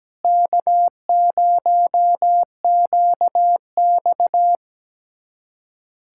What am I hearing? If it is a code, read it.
Morse code, heard as a keyed tone, about 17 wpm, 700 Hz. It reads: K0QX
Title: USA Callsign DE Callsign - 4 Characters - 17wpm